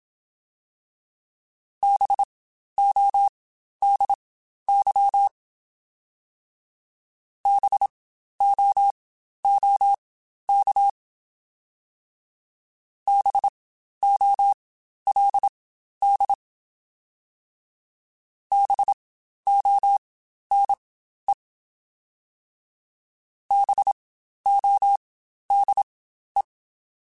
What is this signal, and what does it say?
Morse code, heard as a keyed tone, about 27 wpm, spaced as for 12 wpm, 800 Hz. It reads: BODY BOOK BOLD BONE BODE
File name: Spaced Alphas / Spaced Alphas mp3s / Bs sp @25